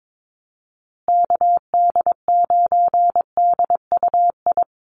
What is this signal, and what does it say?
Morse code, heard as a keyed tone, about 22 wpm, 700 Hz. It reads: KD9DUI
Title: USA Call Signs Repeated 3x 22wpm